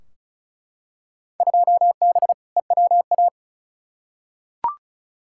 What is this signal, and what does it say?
Morse code, heard as a keyed tone, about 36 wpm, 690 Hz. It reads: 2B EWA
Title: Field Day Exchanges Encore - 35wpm